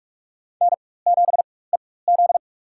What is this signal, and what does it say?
Morse code, heard as a keyed tone, about 41 wpm, spaced as for 20 wpm, 700 Hz. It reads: N7EZ
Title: USA Call Signs - 4 Characters - 4x Character Spacing - 45wpm